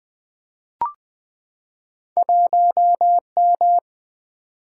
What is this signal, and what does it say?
Morse code, heard as a keyed tone, about 20 wpm, 700 Hz. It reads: 1 M
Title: Single-Letter Single-Number 20wpm